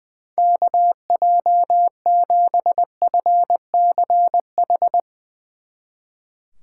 Morse code, heard as a keyed tone, about 22 wpm, 700 Hz. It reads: KJ7FCH
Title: Speed Racing - USA 2x3 Call Signs 20wpm